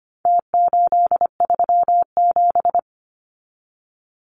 Morse code, heard as a keyed tone, about 25 wpm, 700 Hz. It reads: T837
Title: Numbers - Contest Style - 4-Digit 25wpm